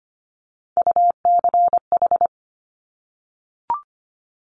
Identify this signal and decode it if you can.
Morse code, heard as a keyed tone, about 25 wpm, 700 Hz. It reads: UCH
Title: Top Three Letter Combinations in Words 25wpm